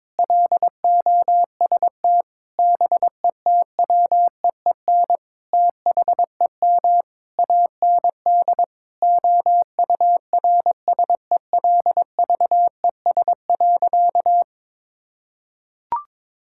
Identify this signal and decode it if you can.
Morse code, heard as a keyed tone, about 22 wpm, 700 Hz. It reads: LOST BETWEEN THEM AND OURSELVES.
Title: Sentences from Top 1600 Words 22wpm